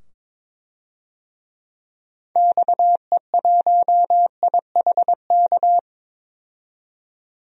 Morse code, heard as a keyed tone, about 22 wpm, 700 Hz. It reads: XE1IHK
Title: DXCC-100 Call Signs - 6 Characters - 22wpm